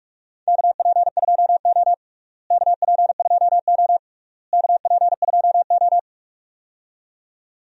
Morse code, heard as a keyed tone, about 46 wpm, 700 Hz. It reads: KP2O KP2O KP2O
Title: DXCC-100 Call Signs - 4 Characters - Repeated 3x - 45wpm